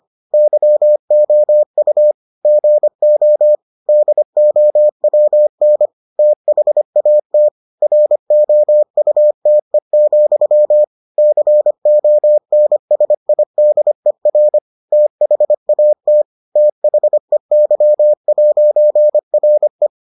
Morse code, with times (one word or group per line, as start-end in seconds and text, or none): 0.3-2.1 YOU
2.5-3.5 GO
3.9-5.9 DOWN
6.2-7.5 THAT
7.8-10.8 ROUTE,
11.2-14.6 CONSIDER
14.9-16.2 THAT
16.6-19.9 THEY'RE